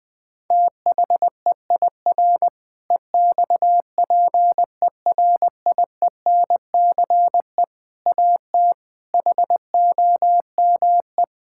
Morse code, heard as a keyed tone, about 20 wpm, 700 Hz. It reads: THEIR EXPERIENCE AT HOME
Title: Sentences from Top 1000 Words - Review Part 6 - 20wpm